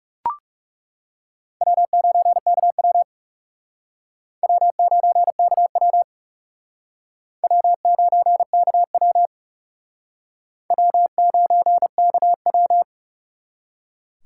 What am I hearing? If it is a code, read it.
Morse code, heard as a keyed tone, about 38 wpm, 700 Hz. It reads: W9KW W9KW W9KW W9KW